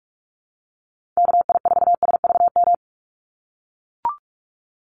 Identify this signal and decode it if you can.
Morse code, heard as a keyed tone, about 44 wpm, 700 Hz. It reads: KI4SVM
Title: NA SOTA Chaser Callsigns 45wpm